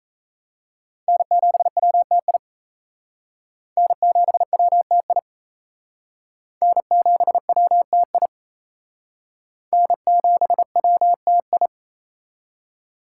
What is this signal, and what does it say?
Morse code, heard as a keyed tone, about 35 wpm, 700 Hz. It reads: N7WTI N7WTI N7WTI N7WTI
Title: Speed Racing - USA 1x3 Call Signs 28wpm